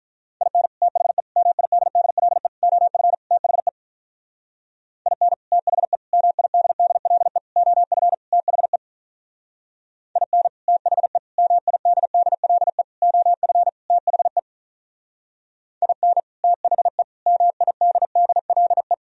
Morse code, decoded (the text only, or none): IN THE MIDDLE OF THE IN THE MIDDLE OF THE IN THE MIDDLE OF THE IN THE MIDDLE